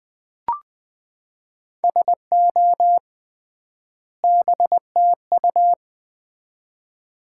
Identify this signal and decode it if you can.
Morse code, heard as a keyed tone, about 20 wpm, 700 Hz. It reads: SO BTU
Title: Common QSO Phrases - 3x Word-Spacing - 20wpm